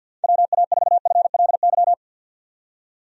Morse code, heard as a keyed tone, about 46 wpm, 700 Hz. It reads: WA4FLY
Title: USA Call Signs - 6 Characters - 50wpm